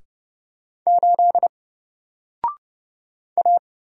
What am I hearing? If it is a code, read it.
Morse code, heard as a keyed tone, about 29 wpm, 700 Hz. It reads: 8 A
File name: Single-Letter-Number-30wpm